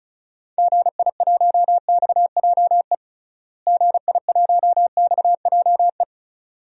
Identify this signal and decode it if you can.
Morse code, heard as a keyed tone, about 34 wpm, 700 Hz. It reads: GI1XJE GI1XJE
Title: DXCC-100 Call Signs - 6 Characters - Repeated 2x - 35wpm